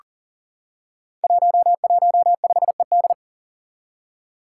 Morse code, heard as a keyed tone, about 38 wpm, 700 Hz. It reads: 115ED
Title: Five Challenging Letter-Number Combinations 40wpm